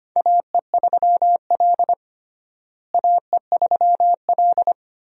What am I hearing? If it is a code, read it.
Morse code, heard as a keyed tone, about 25 wpm, 700 Hz. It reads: AE3L AE3L
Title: USA Call Signs - 4 Characters - Repeated 3x - 25wpm